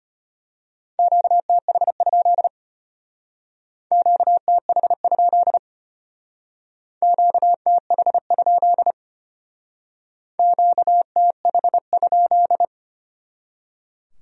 Morse code, heard as a keyed tone, about 32 wpm, 700 Hz. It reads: QTH? QTH? QTH? QTH?